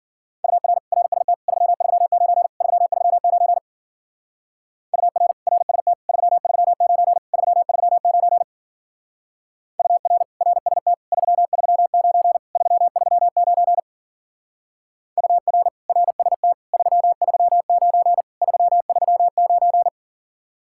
Morse code, heard as a keyed tone, about 48 wpm, 700 Hz. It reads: UR RST 339 339 UR RST 339 339 UR RST 339 339 UR RST 339 339
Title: Speed-Racing Common QSO Phrases - 40wpm